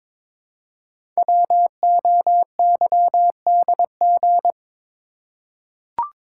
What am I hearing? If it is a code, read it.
Morse code, heard as a keyed tone, about 22 wpm, 700 Hz. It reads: WOYDG
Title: Code Groups - 5 Letters - 22wpm